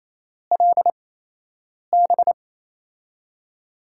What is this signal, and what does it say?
Morse code, heard as a keyed tone, about 27 wpm, spaced as for 8 wpm, 700 Hz. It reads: LB
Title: USA Call Signs - 6 Characters - 8x Character Spacing - 28wpm